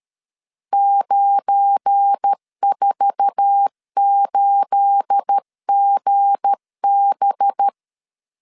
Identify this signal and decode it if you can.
Morse code, heard as a keyed tone, about 12 wpm, 800 Hz. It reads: 948GB
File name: LEZIONI / 10 SIGNS / 10C DOT 12wpm 2451